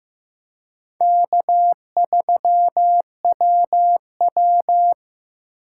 Morse code, heard as a keyed tone, about 15 wpm, 700 Hz. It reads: K3WW
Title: USA Call Signs - 3x Character Spacing - 15wpm